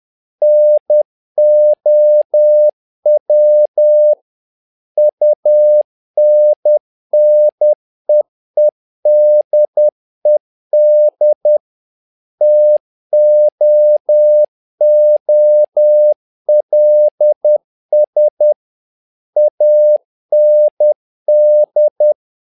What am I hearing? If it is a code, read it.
Morse code, heard as a keyed tone, about 10 wpm, 600 Hz. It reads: NOW UNNEEDED TOOLS AND
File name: learning center / tower trouble / tower trouble 10wpm